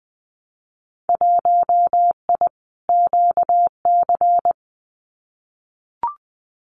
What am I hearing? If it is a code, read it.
Morse code, heard as a keyed tone, about 20 wpm, 700 Hz. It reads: 1I QC
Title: Winter Field Day Exchanges 20wpm